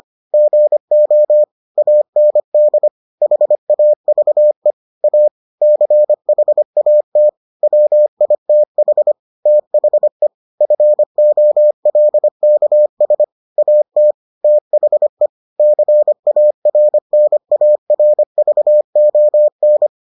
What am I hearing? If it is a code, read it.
Morse code, heard as a keyed tone, about 25 wpm, 600 Hz. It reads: GO AND HAVE A CHAT WITH THE FOLKS AT THE CARNARVON